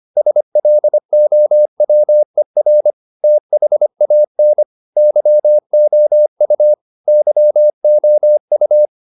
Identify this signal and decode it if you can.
Morse code, heard as a keyed tone, about 25 wpm, 600 Hz. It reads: SLOWER THAN YOU YOU